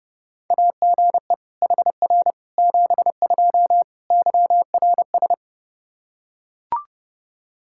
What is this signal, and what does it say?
Morse code, heard as a keyed tone, about 30 wpm, 700 Hz. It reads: AGE HR 72 YRS
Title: K1USN SST QSO Practice 30wpm